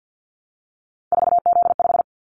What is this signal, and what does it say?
Morse code, heard as a keyed tone, about 50 wpm, 700 Hz. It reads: 4Z5
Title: Callsign Prefixes for Top 50 DXCC - 50wpm